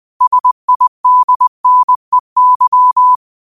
Morse code, heard as a keyed tone, about 20 wpm, 1000 Hz. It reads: SIDNEY
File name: qotdfiles / 20wpm